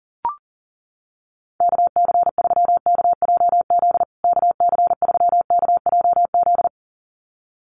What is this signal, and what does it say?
Morse code, heard as a keyed tone, about 40 wpm, 700 Hz. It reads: KC3KJZ KC3KJZ
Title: USA Call Signs Repeated 2x 40wpm